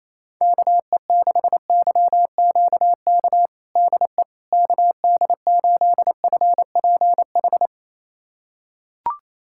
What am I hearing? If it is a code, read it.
Morse code, heard as a keyed tone, about 28 wpm, 700 Hz. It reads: KE6YQK DE KD8FPH
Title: USA Callsign DE Callsign 28wpm